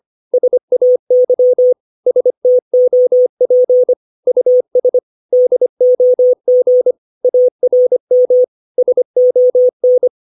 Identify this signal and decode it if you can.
Morse code, heard as a keyed tone, about 25 wpm, 500 Hz. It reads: SAY STOP US DOG ARM SON
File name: 632 words / 632 Words 25WPM